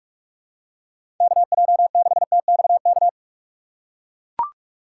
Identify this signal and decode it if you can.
Morse code, heard as a keyed tone, about 42 wpm, 700 Hz. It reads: KJ6TXK
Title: USA Call Signs - 6 Characters - 4x Character Spacing - 45wpm